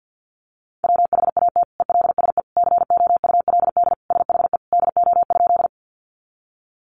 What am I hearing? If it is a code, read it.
Morse code, heard as a keyed tone, about 50 wpm, 700 Hz. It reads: WHAT ELSE COULD SHE DO?
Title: Sentences from Top 800 Words 50wpm